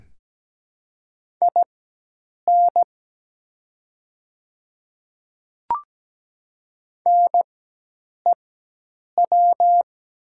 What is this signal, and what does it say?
Morse code, heard as a keyed tone, about 17 wpm, spaced as for 8 wpm, 700 Hz. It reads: IN NEW
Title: 4x Character Spacing - Top 300 Words - 17wpm